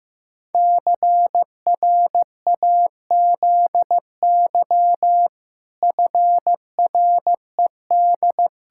Morse code, heard as a keyed tone, about 15 wpm, 700 Hz. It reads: CRAZY FRED
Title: Pangram Challenge v2 - 15wpm